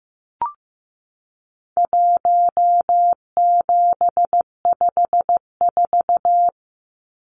Morse code, heard as a keyed tone, about 15 wpm, 700 Hz. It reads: 1754
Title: Numbers - Contest Style - 4-Digit 15wpm